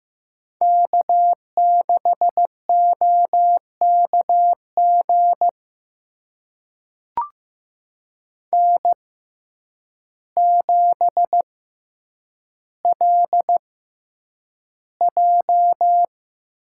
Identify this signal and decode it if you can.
Morse code, heard as a keyed tone, about 15 wpm, spaced as for 8 wpm, 700 Hz. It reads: K6OKG N7LJ
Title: USA Call Signs - 5 Characters - 6x Character Spacing - 15wpm